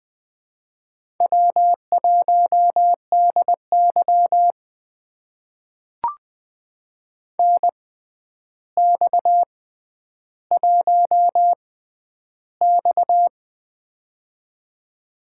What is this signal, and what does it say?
Morse code, heard as a keyed tone, about 20 wpm, spaced as for 10 wpm, 700 Hz. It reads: W1DY NX1X
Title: USA Call Signs - 4 Characters - 6x Character Spacing - 20wpm